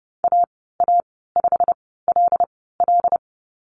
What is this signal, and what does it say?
Morse code, heard as a keyed tone, about 32 wpm, spaced as for 17 wpm, 700 Hz. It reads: AA5LL
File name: Lesson 15g - Number 5 - Call Signs 30wpm